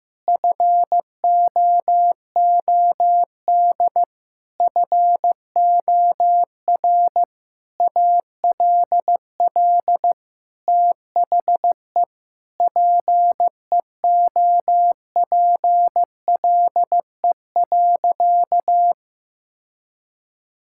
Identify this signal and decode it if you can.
Morse code, heard as a keyed tone, about 15 wpm, 700 Hz. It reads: FOOD FOR ALL THE PEOPLE.
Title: Sentences from Top 1600 Words 15wpm